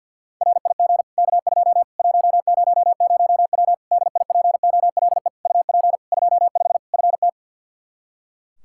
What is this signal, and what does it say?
Morse code, heard as a keyed tone, about 52 wpm, 700 Hz. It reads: RIG K2 100W DIPOLE UP 25 FT
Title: Practice QSO - Exchange 2 - Info - 50wpm